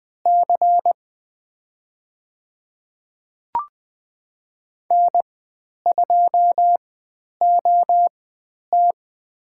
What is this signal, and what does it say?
Morse code, heard as a keyed tone, about 20 wpm, spaced as for 10 wpm, 700 Hz. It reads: C N2OT
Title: 095 Lesson 16g - Letter c - Call Signs 20wpm